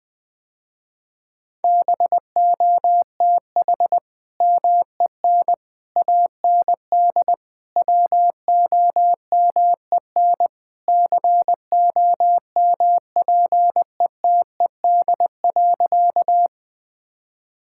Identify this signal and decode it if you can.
Morse code, heard as a keyed tone, about 20 wpm, 700 Hz. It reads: BOTH MEN AND WOMEN COMPETED.
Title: Sentences with 3-Word Phrases - Spoken Beginning - 20wpm